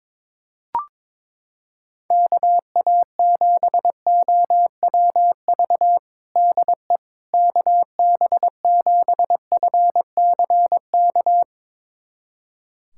Morse code, heard as a keyed tone, about 22 wpm, 700 Hz. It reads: KA7OWV DE KB7FCK
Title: USA Callsign DE Callsign - 6 Characters - 22wpm